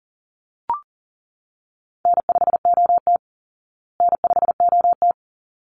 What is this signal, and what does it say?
Morse code, heard as a keyed tone, about 40 wpm, 700 Hz. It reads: N5OT N5OT
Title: Call Signs Repeated 3x - 40wpm